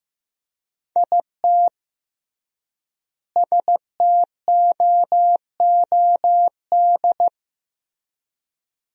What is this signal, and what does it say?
Morse code, heard as a keyed tone, about 15 wpm, 700 Hz. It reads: IT STOOD
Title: 3x Word-Spacing - Sentences from Top 500 Words - 15wpm